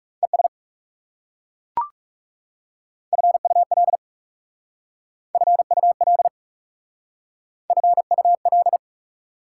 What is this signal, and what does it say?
Morse code, heard as a keyed tone, about 37 wpm, 690 Hz. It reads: ES FUL FUL FUL
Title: Speed-Racing Prefixes and Suffixes 30wpm